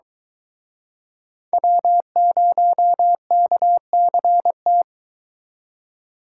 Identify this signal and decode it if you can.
Morse code, heard as a keyed tone, about 24 wpm, 700 Hz. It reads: W0KCT